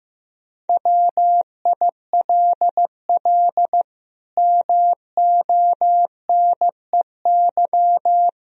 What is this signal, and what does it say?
Morse code, heard as a keyed tone, about 16 wpm, 700 Hz. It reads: WILL MONEY
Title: Sentences from Top 1000 Words - Review Part 4 - 15wpm